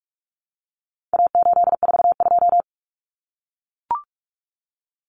Morse code, heard as a keyed tone, about 44 wpm, 700 Hz. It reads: A842
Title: Common Cut Numbers - 4-Digit 45wpm